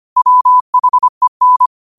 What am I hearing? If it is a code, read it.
Morse code, heard as a keyed tone, about 25 wpm, 1000 Hz. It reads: WHEN
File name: qotdfiles / 25wpm